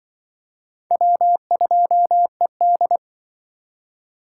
Morse code, heard as a keyed tone, about 25 wpm, 700 Hz. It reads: W2ED
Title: Speed Racing - USA 1x2 Call Signs 20wpm